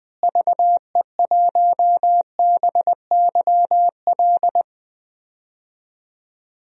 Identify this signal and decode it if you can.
Morse code, heard as a keyed tone, about 20 wpm, 700 Hz. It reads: VE1BYL
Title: DXCC-100 Call Signs - 6 Characters - Repeated 3x - 20wpm